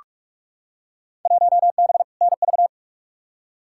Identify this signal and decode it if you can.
Morse code, heard as a keyed tone, about 46 wpm, 700 Hz. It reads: 1B NV
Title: Field Day Exchanges - 45wpm